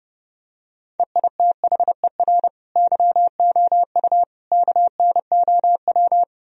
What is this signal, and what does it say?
Morse code, heard as a keyed tone, about 30 wpm, 700 Hz. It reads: EITHER YOU KNOW
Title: Sentences from Top 1000 Words - Review Part 4 - 30wpm